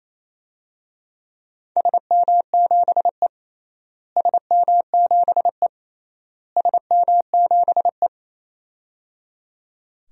Morse code, heard as a keyed tone, about 28 wpm, 700 Hz. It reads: SM7E SM7E SM7E
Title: DXCC-100 Call Signs - 4 Characters - Repeated 3x - 28wpm